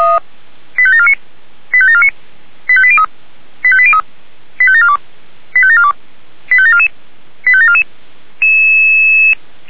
Five-tone selective call (ZVEI), heard as quick runs of five tones, tones of about 70 ms. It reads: 85749 85749 85803 85803 85742 85742 85744 85744